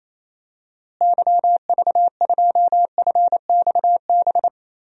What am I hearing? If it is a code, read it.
Morse code, heard as a keyed tone, about 28 wpm, 700 Hz. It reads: YV2FXB
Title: DXCC-100 Call Signs - 6 Characters - Repeated 3x - 28wpm